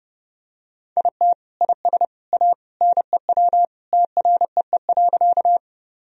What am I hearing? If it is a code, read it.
Morse code, heard as a keyed tone, about 32 wpm, 700 Hz. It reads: IT IS A NEW TREE.
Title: Sentences from Top 1000 Words - Review Part 1 - 30wpm